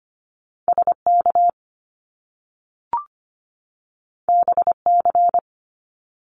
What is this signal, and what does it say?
Morse code, heard as a keyed tone, about 25 wpm, 700 Hz. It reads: SK BC
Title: Canadian Provinces and Territories 25wpm